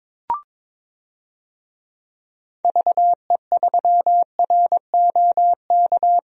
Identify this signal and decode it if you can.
Morse code, heard as a keyed tone, about 22 wpm, 700 Hz. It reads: VE3ROK
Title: DXCC-100 Call Signs - 6 Characters - Repeated 2x - 22wpm